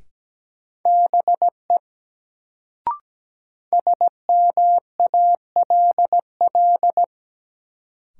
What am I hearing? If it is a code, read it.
Morse code, heard as a keyed tone, about 17 wpm, 700 Hz. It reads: BE SMALL
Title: Sets of 2 Words - New Word Drill - 17wpm